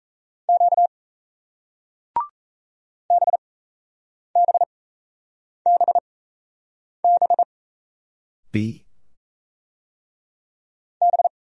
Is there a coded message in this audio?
Yes: Morse code from 0.5 to 11.3 s, heard as a keyed tone, about 36 wpm, spaced as for 8 wpm, 700 Hz. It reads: Q BBBB B